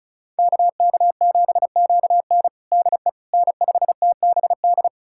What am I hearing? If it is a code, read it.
Morse code, heard as a keyed tone, about 36 wpm, 700 Hz. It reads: KK7QN DE N5TBD
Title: USA Callsign DE Callsign - 5 Characters - 35wpm